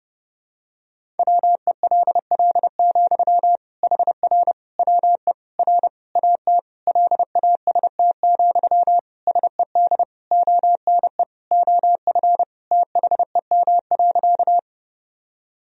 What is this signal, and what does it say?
Morse code, heard as a keyed tone, about 29 wpm, 700 Hz. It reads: WELL, HR WE R AT LAST, SED ONE OF THEM.
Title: Sentences from Top 300 Words and Abbreviations 30wpm